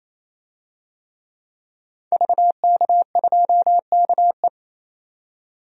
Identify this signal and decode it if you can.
Morse code, heard as a keyed tone, about 28 wpm, 700 Hz. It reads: VK2KE
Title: DXCC-100 Call Signs - 5 Characters - 28wpm